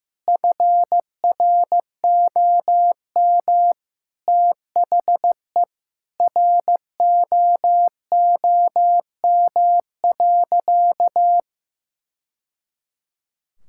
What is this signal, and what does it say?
Morse code, heard as a keyed tone, about 15 wpm, 700 Hz. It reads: FROM THE ROOM.
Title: Sentences from Top 1000 Words - Review Part 5 - 15wpm